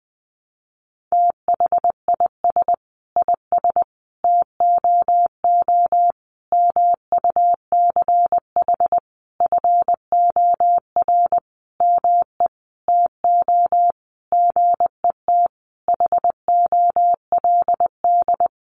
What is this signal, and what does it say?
Morse code, heard as a keyed tone, about 20 wpm, 700 Hz. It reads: THIS IS TOO MUCH FOR ME TO GET HOLD
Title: A Sound Of Thunder 20wpm